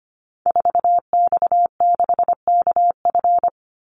Morse code, heard as a keyed tone, about 25 wpm, 700 Hz. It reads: 4X6KF
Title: Call Signs Repeated 2x - 25wpm